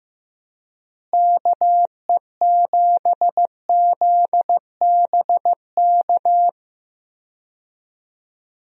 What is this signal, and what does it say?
Morse code, heard as a keyed tone, about 15 wpm, 700 Hz. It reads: KE7ZBK
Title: USA Call Signs - 6 Characters - Repeated 3x - 15wpm